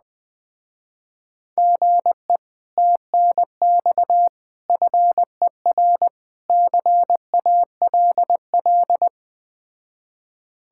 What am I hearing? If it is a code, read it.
Morse code, heard as a keyed tone, about 20 wpm, 700 Hz. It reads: GE TNX FER CALL
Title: Speed-Racing Common QSO Phrases Encore - 17wpm